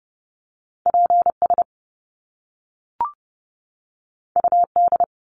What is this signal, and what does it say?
Morse code, heard as a keyed tone, about 30 wpm, 700 Hz. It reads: PS UD
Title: Top Two Letter Combinations in Words 30wpm